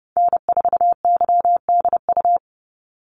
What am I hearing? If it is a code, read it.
Morse code, heard as a keyed tone, about 30 wpm, 700 Hz. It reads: N4YDU
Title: Call Signs Repeated 3x - 30wpm